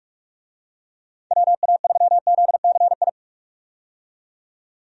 Morse code, heard as a keyed tone, about 42 wpm, 700 Hz. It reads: WA3ZCI
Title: USA Call Signs - 6 Characters - 45wpm